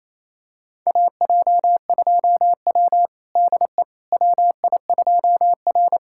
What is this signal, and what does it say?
Morse code, heard as a keyed tone, about 28 wpm, 700 Hz. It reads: AJ2W DE WI2R
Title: USA Callsign DE Callsign - 4 Characters - 28wpm